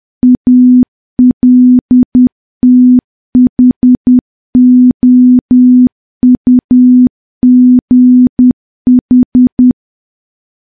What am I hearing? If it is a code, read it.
Morse code, heard as a keyed tone, about 10 wpm, 250 Hz. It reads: ALTHOUGH
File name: morse / morse-training-10WPM